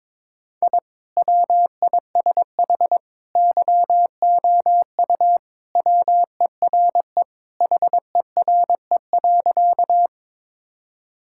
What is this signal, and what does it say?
Morse code, heard as a keyed tone, about 22 wpm, 700 Hz. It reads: I WISH YOU WERE HERE.
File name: practice / 22wpm / Sentences with 2-Word Phrases - 22wpm